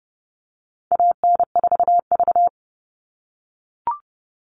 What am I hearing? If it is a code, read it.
Morse code, heard as a keyed tone, about 30 wpm, 700 Hz. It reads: AN4V
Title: All Cut Numbers - 4-Digit 30wpm